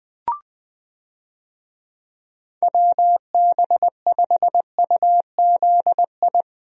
Morse code, heard as a keyed tone, about 19 wpm, 700 Hz. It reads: WB5UZI
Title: USA Call Signs - 6 Characters - Repeated 3x - 20wpm